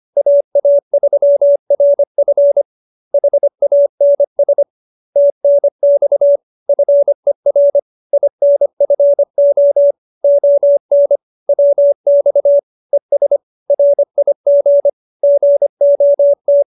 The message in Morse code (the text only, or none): AA3RF HANS TNX FER INFO ON WX ES RIG GOT